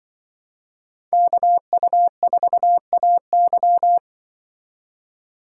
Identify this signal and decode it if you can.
Morse code, heard as a keyed tone, about 24 wpm, 700 Hz. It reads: KU4AY
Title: Speed Racing - USA 2x2 Call Signs 20wpm